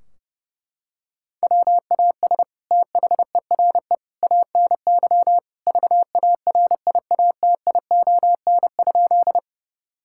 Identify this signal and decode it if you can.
Morse code, heard as a keyed tone, about 30 wpm, 700 Hz. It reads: WAS THERE ANY VARIATION?
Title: Sentences from Top 1600 Words 30wpm